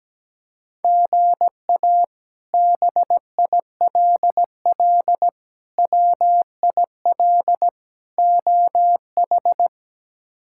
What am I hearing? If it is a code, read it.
Morse code, heard as a keyed tone, about 17 wpm, 700 Hz. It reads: GA BILL WIL OH
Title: K1USN SST QSO Practice - 17wpm